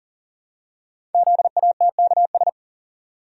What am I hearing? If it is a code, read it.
Morse code, heard as a keyed tone, about 40 wpm, 700 Hz. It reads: ZATKS
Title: Code Groups - 5 Letters - 40wpm